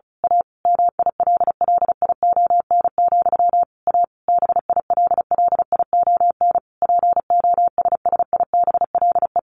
Morse code, heard as a keyed tone, about 34 wpm, 700 Hz. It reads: A MILLION, A BILLION POSSIBLE
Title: A Sound Of Thunder 35wpm